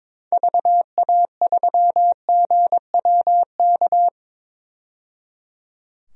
Morse code, heard as a keyed tone, about 22 wpm, 700 Hz. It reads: VA3GWK
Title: DXCC-100 Call Signs - 6 Characters - 22wpm